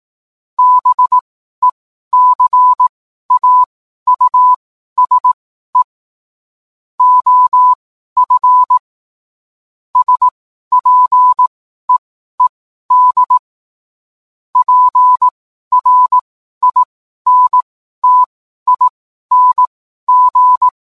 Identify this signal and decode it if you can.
Morse code, heard as a keyed tone, about 18 wpm, spaced as for 12 wpm, 1000 Hz. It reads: BECAUSE OF SPEED PRINTING
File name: cwblog / cw mp3 directory / mem morse 12